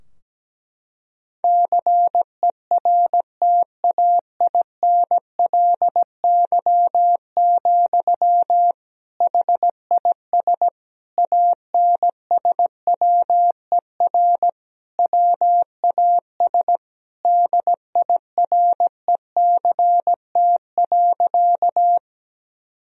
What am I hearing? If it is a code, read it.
Morse code, heard as a keyed tone, about 17 wpm, 700 Hz. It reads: CERTAINLY, HIS ANSWER WAS DIRECT.
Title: Sentences from Top 1000 Words - Review Part 2 - 17wpm